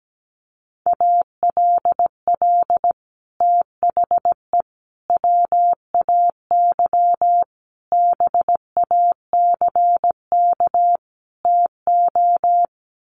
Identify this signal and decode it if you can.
Morse code, heard as a keyed tone, about 17 wpm, 700 Hz. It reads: ALL THE WAY BACK TO
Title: Sets of 5 Words Encore 17wpm